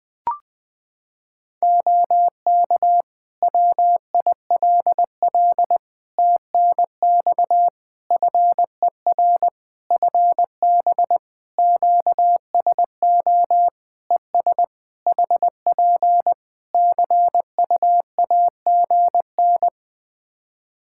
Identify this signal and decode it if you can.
Morse code, heard as a keyed tone, about 21 wpm, 700 Hz. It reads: OK WILL TNX FER FB QSO ES HP CUAGN
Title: Practice QSO - Exchange 4 - Ending - 20wpm